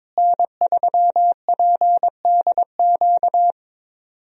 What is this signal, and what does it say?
Morse code, heard as a keyed tone, about 22 wpm, 700 Hz. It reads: N3PDQ
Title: USA Call Signs - 5 Characters - Repeated 2x - 22wpm